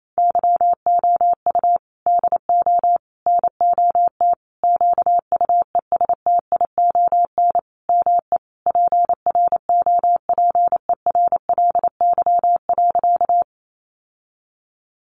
Morse code, heard as a keyed tone, about 28 wpm, 700 Hz. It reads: YOU DO NOT QUESTION ME PROPERLY.
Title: Sentences from Top 1100 Words 28wpm